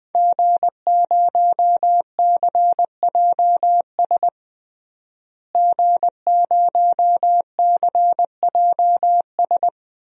Morse code, heard as a keyed tone, about 20 wpm, 700 Hz. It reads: G0CJS G0CJS
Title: DXCC-100 Call Signs - 5 Characters - Repeated 2x - 20wpm